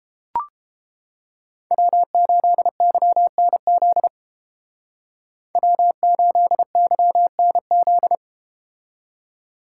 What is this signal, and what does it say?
Morse code, heard as a keyed tone, about 29 wpm, 700 Hz. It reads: W8YNZ W8YNZ